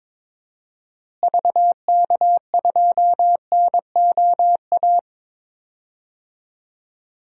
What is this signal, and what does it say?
Morse code, heard as a keyed tone, about 22 wpm, 700 Hz. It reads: VK2NOA
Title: DXCC-100 Call Signs - 6 Characters - Repeated 2x - 22wpm